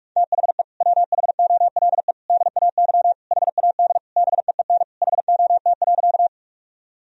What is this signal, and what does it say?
Morse code, heard as a keyed tone, about 46 wpm, 700 Hz. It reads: THE WHOLE DAY HAD BEEN HOT.
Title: Sentences from Top 1000 Words - Review Part 3 - 45wpm